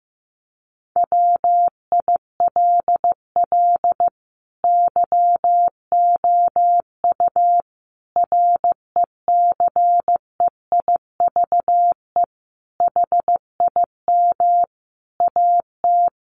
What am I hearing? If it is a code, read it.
Morse code, heard as a keyed tone, about 15 wpm, 700 Hz. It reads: WILL YOU RECEIVE HIM AT
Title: Sentences from Top 1000 Words 15wpm